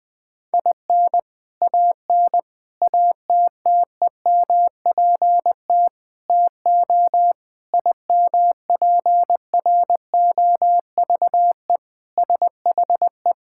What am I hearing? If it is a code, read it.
Morse code, heard as a keyed tone, about 20 wpm, 700 Hz. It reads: IN AN ATTEMPT TO IMPROVE SHE
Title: Sentences with 4-Word Phrases - 20wpm